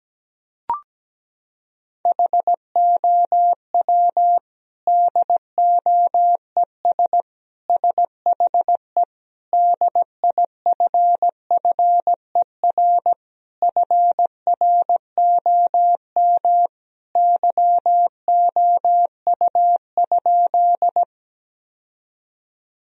Morse code, heard as a keyed tone, about 17 wpm, 700 Hz. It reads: HOW DOES SHE DIFFER FROM YOU?
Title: Sentences from Top 1000 Words - Review Part 1 - 17wpm